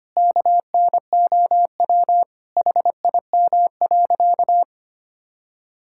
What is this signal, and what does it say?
Morse code, heard as a keyed tone, about 25 wpm, 700 Hz. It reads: KNOW HIM.
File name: practice / 25wpm / Sentences with 4-Word Phrases - 25wpm